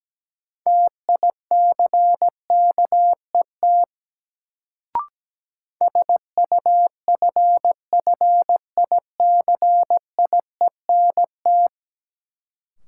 Morse code, heard as a keyed tone, about 17 wpm, 700 Hz. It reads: TICKET SUFFICIENT
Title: Sentences from Top 1400 Words - New Word Drill Alt - 17wpm